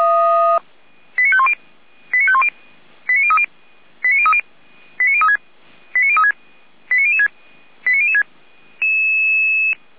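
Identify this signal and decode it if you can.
Five-tone selective call (ZVEI), heard as quick runs of five tones, tones of about 70 ms. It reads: 89510 89510 89030 89030 89036 89036 89006 89006
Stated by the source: ZVEI five-tone selective call received on an SDR, archived by Feuerwehr Orlamünde fire brigade.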